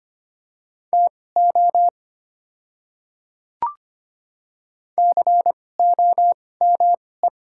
Morse code, heard as a keyed tone, about 25 wpm, spaced as for 18 wpm, 700 Hz. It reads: TO COME
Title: 2x Character Spacing - Top 200 Words - 25wpm